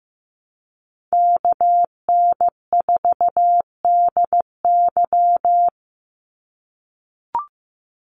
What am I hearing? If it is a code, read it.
Morse code, heard as a keyed tone, about 15 wpm, 700 Hz. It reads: KN4DY
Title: USA Call Signs 15wpm